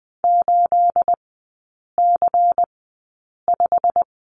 Morse code, heard as a keyed tone, about 20 wpm, spaced as for 8 wpm, 690 Hz. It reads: 8C5
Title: 212 Lesson 34e - Number 8 - Three Characters 20wpm